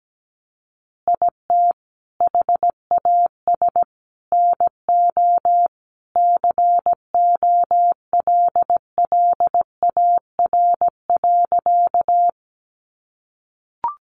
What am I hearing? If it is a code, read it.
Morse code, heard as a keyed tone, about 17 wpm, 700 Hz. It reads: IT HAS NO COLLAR.
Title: Easy Phrases 17wpm